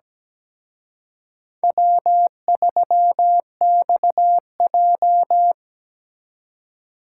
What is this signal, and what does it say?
Morse code, heard as a keyed tone, about 20 wpm, 700 Hz. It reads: W3XJ
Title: Speed Racing - USA 1x2 Call Signs 17wpm